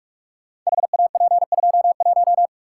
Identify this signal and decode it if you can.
Morse code, heard as a keyed tone, about 44 wpm, 700 Hz. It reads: HAP21
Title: Code Groups - 5 Letter:Numbers - 45wpm